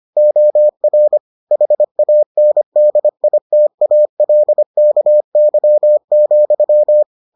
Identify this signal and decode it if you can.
Morse code, heard as a keyed tone, about 25 wpm, 600 Hz. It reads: OR HANDITALKY,